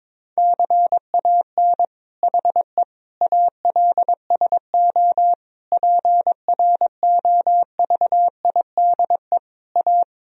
Morse code, heard as a keyed tone, about 22 wpm, 700 Hz. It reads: CAN HE ALSO PROVIDE A
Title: Sentences from Top 1000 Words - Review Part 3 - 22wpm